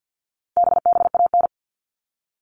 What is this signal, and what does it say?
Morse code, heard as a keyed tone, about 50 wpm, 700 Hz. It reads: BBAN
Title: All Cut Numbers - 4-Digit 50wpm